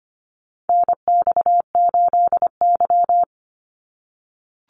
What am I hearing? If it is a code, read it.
Morse code, heard as a keyed tone, about 25 wpm, 700 Hz. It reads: NX8Y
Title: Call Signs - 25wpm